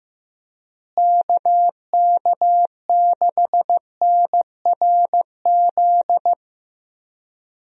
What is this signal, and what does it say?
Morse code, heard as a keyed tone, about 15 wpm, 700 Hz. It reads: KK6NRZ
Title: USA Call Signs - 6 Characters - Repeated 2x - 15wpm